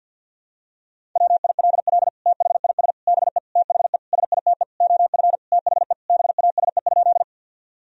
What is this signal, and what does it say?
Morse code, heard as a keyed tone, about 53 wpm, 700 Hz. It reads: WILL THIS BE THE SITE OF THE BASE?
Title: Sentences from Top 1000 Words - Review Part 1 - 50wpm